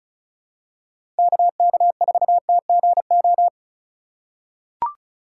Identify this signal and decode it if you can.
Morse code, heard as a keyed tone, about 36 wpm, 700 Hz. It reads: KK4TGO
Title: USA Call Signs - 6 Characters - 3x Character Spacing - 35wpm